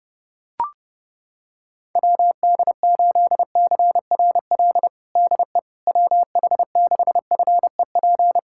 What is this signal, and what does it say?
Morse code, heard as a keyed tone, about 30 wpm, 700 Hz. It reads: WD8CRL DE WH6FEP
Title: USA Callsign DE Callsign - 6 Characters - 30wpm